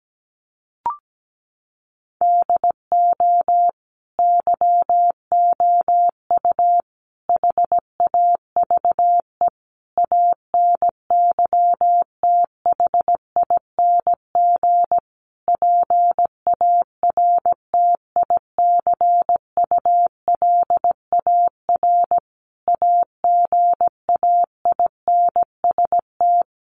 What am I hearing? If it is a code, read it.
Morse code, heard as a keyed tone, about 17 wpm, 700 Hz. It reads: DO YOU HAVE ANYTHING PARTICULAR AGAINST
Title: Sentences from Top 1000 Words 17wpm